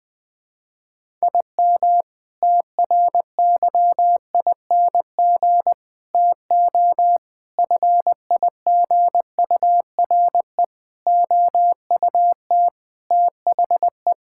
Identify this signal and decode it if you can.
Morse code, heard as a keyed tone, about 20 wpm, 700 Hz. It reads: IM TRYING TO FIGURE OUT THE
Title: Sentences with 4-Word Phrases - 20wpm